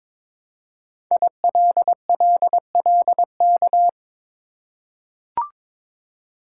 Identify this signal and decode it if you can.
Morse code, heard as a keyed tone, about 22 wpm, 700 Hz. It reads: ILLLK
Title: Code Groups - 5 Letters - 22wpm